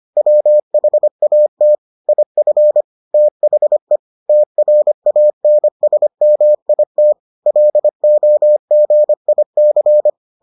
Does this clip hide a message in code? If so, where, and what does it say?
Yes: Morse code from 0.2 to 10.1 s, heard as a keyed tone, about 25 wpm, 600 Hz. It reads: WHAT IF THE TRANSMIT LOGIC